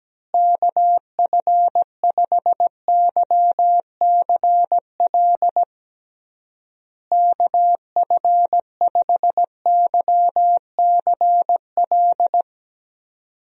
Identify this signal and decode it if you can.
Morse code, heard as a keyed tone, about 17 wpm, 700 Hz. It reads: KF5YCL KF5YCL
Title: USA Call Signs - 6 Characters - Repeated 3x - 17wpm